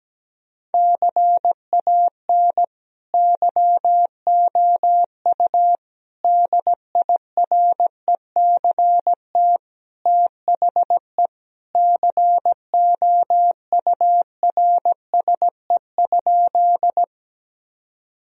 Morse code, heard as a keyed tone, about 17 wpm, 700 Hz. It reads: CAN YOU DIRECT THE COURSE?
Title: Sentences from Top 1000 Words - Review Part 2 - 17wpm